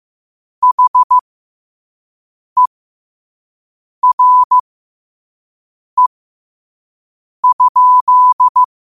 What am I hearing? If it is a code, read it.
Morse code, heard as a keyed tone, about 15 wpm, spaced as for 5 wpm, 1000 Hz. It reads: HERE?
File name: qotdfiles / 5wpm